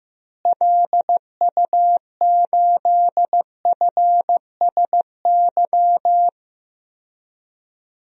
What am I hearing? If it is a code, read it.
Morse code, heard as a keyed tone, about 15 wpm, 700 Hz. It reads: LU8FSY
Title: DXCC-100 Call Signs - 6 Characters - Repeated 2x - 15wpm